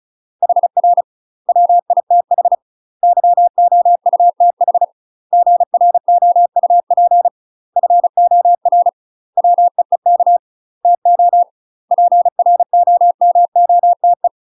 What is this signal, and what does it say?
Morse code, heard as a keyed tone, about 34 wpm, 700 Hz. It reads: HR WITH YOUTH GROUP FOR WEEK TO PROMOTE
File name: UK MASTER QSOs TRAINING / 35WPM / UK MASTER QSOs 0034